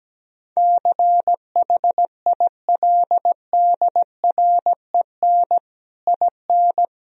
Morse code, heard as a keyed tone, about 17 wpm, 700 Hz. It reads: CHILDREN IN
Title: Sentences from Top 1000 Words - Review Part 4 - 17wpm